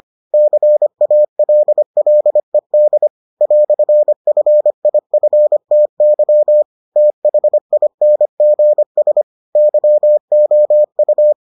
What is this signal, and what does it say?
Morse code, heard as a keyed tone, about 25 wpm, 600 Hz. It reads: CALLED "FIFTY THINGS YOU